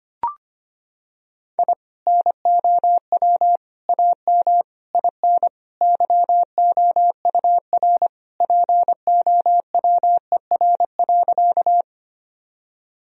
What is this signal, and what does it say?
Morse code, heard as a keyed tone, about 25 wpm, 700 Hz. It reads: I NOW AM IN YOUR POWER.
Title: Sentences from Top 1000 Words - Review Part 2 - 25wpm